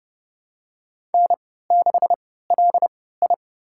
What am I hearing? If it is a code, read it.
Morse code, heard as a keyed tone, about 30 wpm, spaced as for 17 wpm, 700 Hz. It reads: N6LI
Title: USA Call Signs - 4 Characters - 3x Character Spacing - 30wpm